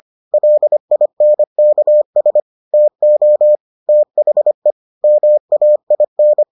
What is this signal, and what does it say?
Morse code, heard as a keyed tone, about 25 wpm, 600 Hz. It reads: LINKS TO THE MAIN